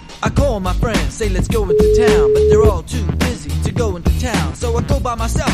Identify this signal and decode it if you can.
Ringback tone (Europe, 425 Hz), with noise.